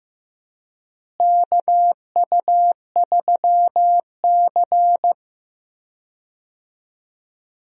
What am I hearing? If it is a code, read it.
Morse code, heard as a keyed tone, about 15 wpm, 700 Hz. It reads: KU3C
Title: USA Call Signs - 4 Characters - Repeated 3x - 15wpm